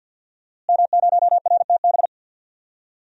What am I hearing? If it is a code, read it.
Morse code, heard as a keyed tone, about 49 wpm, 700 Hz. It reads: M0RTB
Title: DXCC-100 Call Signs - 5 Characters - Repeated 2x - 50wpm